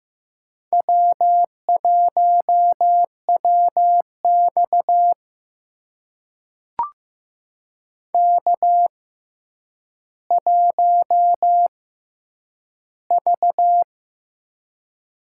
Morse code, heard as a keyed tone, about 15 wpm, spaced as for 9 wpm, 700 Hz. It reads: W1WX K1V
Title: USA Call Signs - 6x Character Spacing - 15wpm